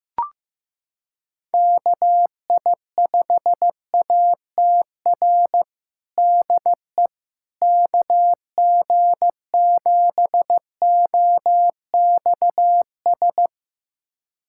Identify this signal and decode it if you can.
Morse code, heard as a keyed tone, about 15 wpm, 700 Hz. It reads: KI5ATR DE KG7OXS
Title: USA Callsign DE Callsign 15wpm